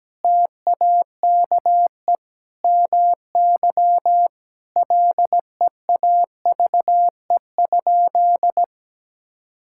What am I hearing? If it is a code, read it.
Morse code, heard as a keyed tone, about 17 wpm, 700 Hz. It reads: TAKE MY LEAVE?
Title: Sentences from Top 1000 Words - Review Part 5 - 17wpm